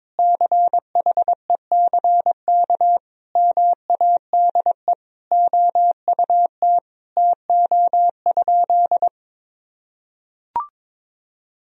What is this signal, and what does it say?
Morse code, heard as a keyed tone, about 22 wpm, 700 Hz. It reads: CHECK MADE OUT TO?
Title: Sentences from Top 1000 Words - Review Part 3 - 22wpm